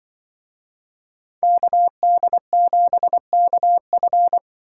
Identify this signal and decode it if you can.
Morse code, heard as a keyed tone, about 25 wpm, 700 Hz. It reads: KD7KF
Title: Speed Racing - USA Call Signs 20wpm